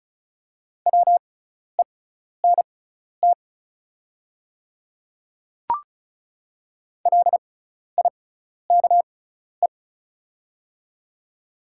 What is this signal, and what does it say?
Morse code, heard as a keyed tone, about 36 wpm, spaced as for 12 wpm, 690 Hz. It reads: WENT LIKE
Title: 6x Character Spacing - Top 200 Words - 35wpm